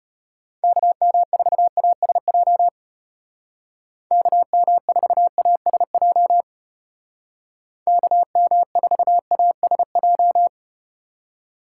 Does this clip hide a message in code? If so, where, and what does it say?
Yes: Morse code from 0.6 to 10.5 s, heard as a keyed tone, about 33 wpm, 700 Hz. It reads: KM4ASJ KM4ASJ KM4ASJ